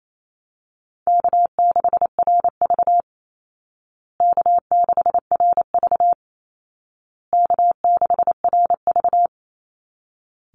Morse code, heard as a keyed tone, about 28 wpm, 700 Hz. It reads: K6RV K6RV K6RV
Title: Call Signs Repeated 2x - 28wpm